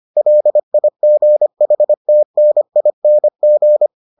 Morse code, heard as a keyed tone, about 25 wpm, 600 Hz. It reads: LIGHTNING